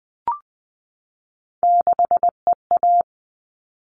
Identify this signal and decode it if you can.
Morse code, heard as a keyed tone, about 20 wpm, 700 Hz. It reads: 6EA
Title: All Cut Numbers - 3-Digit 20wpm